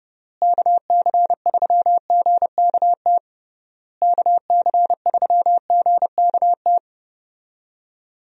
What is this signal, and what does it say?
Morse code, heard as a keyed tone, about 30 wpm, 700 Hz. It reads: KC3GKT KC3GKT
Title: USA Call Signs - 6 Characters - Repeated 3x - 30wpm